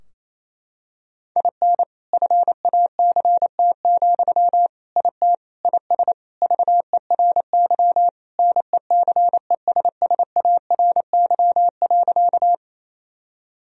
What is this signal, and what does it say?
Morse code, heard as a keyed tone, about 27 wpm, 700 Hz. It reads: IN FACT, IT IS VERY NECESSARY.
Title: Sentences from Top 1000 Words - Review Part 3 - 28wpm